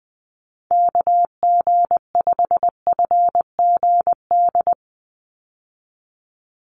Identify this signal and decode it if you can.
Morse code, heard as a keyed tone, about 20 wpm, 700 Hz. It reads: KG5FGD
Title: USA Call Signs 20wpm